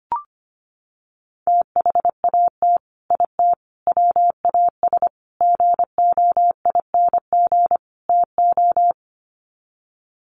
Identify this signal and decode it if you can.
Morse code, heard as a keyed tone, about 25 wpm, 700 Hz. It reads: THAT IT WAS GOING TO
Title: Sets of 5 Words Encore2 25wpm